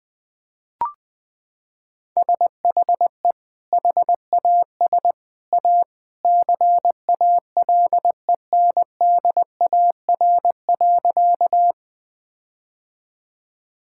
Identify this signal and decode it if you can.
Morse code, heard as a keyed tone, about 21 wpm, 700 Hz. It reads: SHE HAS A CALENDAR.
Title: Sentences from Top 1600 Words 20wpm